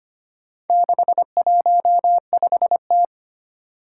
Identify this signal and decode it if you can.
Morse code, heard as a keyed tone, about 25 wpm, 700 Hz. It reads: 615T
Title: Four Challenging Letter-Number Combinations 25wpm